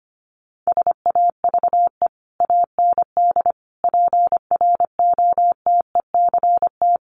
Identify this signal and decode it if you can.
Morse code, heard as a keyed tone, about 25 wpm, 700 Hz. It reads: SAVE AND PROTECT
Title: Sentences from Top 1100 Words 25wpm